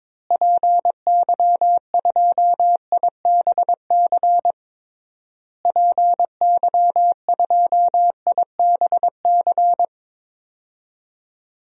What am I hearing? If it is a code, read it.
Morse code, heard as a keyed tone, about 22 wpm, 700 Hz. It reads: PY2IBC PY2IBC
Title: DXCC-100 Call Signs - 6 Characters - Repeated 3x - 22wpm